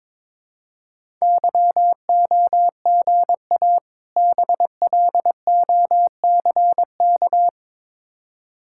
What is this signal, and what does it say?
Morse code, heard as a keyed tone, about 22 wpm, 700 Hz. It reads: YOGA BLOCK
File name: Difficult Words with Hints - 22wpm